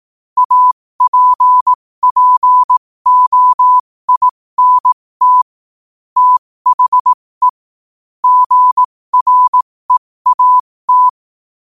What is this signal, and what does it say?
Morse code, heard as a keyed tone, about 18 wpm, spaced as for 15 wpm, 1000 Hz. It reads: APPOINT THE GREAT
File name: qotdfiles / 15wpm